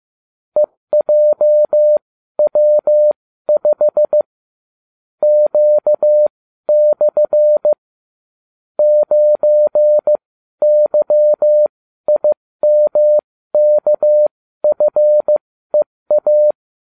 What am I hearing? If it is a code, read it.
Morse code, heard as a keyed tone, about 15 wpm, spaced as for 12 wpm, 600 Hz. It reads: EJW5 Q/ 9YIMKFEA